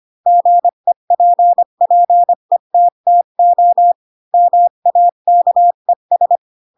Morse code, heard as a keyed tone, about 25 wpm, 700 Hz. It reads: GEPPETTO MAKES
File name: Pinocchio / Pinocchio ch8 20 25 ik1hge